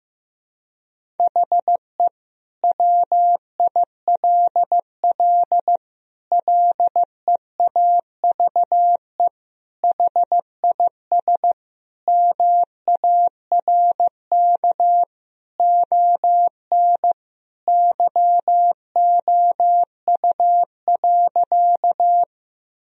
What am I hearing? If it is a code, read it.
Morse code, heard as a keyed tone, about 15 wpm, 700 Hz. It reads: HE WILL LEAVE HIS MARK ON YOU.
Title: Sentences from Top 1000 Words - Review Part 1 - 15wpm